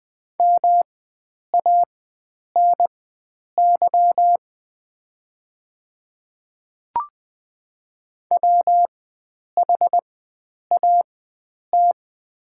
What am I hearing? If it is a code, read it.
Morse code, heard as a keyed tone, about 20 wpm, spaced as for 9 wpm, 690 Hz. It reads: MANY WHAT
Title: 4x Character Spacing - Top 100 Words - 20wpm